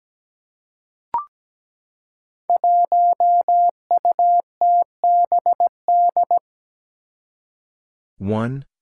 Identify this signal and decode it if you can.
Morse code, heard as a keyed tone, about 17 wpm, 700 Hz. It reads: 1UTBD
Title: Five Challenging Letter-Number Combinations 17wpm